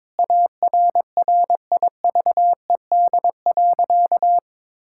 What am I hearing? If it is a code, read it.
Morse code, heard as a keyed tone, about 22 wpm, 700 Hz. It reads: ARRIVED.
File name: practice / 22wpm / Sentences with 7-Word Phrases - 22wpm